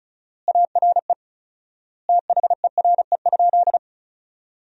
Morse code, heard as a keyed tone, about 34 wpm, 700 Hz. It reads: ARE THERE?
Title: 4x Word-Spacing - Sentences from Top 100 Words - 35wpm